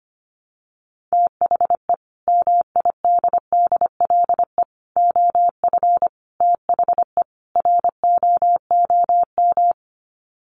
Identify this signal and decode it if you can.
Morse code, heard as a keyed tone, about 25 wpm, 700 Hz. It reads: THE MIDDLE OF THE ROOM
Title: Sets of 5 Words Encore2 25wpm